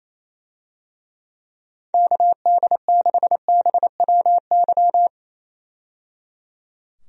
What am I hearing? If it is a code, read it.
Morse code, heard as a keyed tone, about 27 wpm, 700 Hz. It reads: KD6BWY
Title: USA Call Signs - 6 Characters - 28wpm